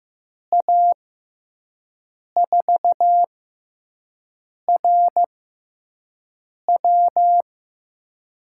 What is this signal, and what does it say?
Morse code, heard as a keyed tone, about 15 wpm, spaced as for 5 wpm, 700 Hz. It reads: A4RW
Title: USA Call Signs - 5 Characters - 6x Character Spacing - 15wpm